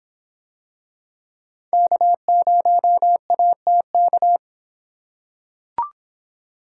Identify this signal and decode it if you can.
Morse code, heard as a keyed tone, about 23 wpm, 700 Hz. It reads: K0ATK